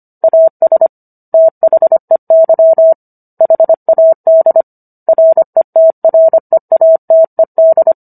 Morse code, heard as a keyed tone, about 25 wpm, 650 Hz. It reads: AS THEY HAD RETREATED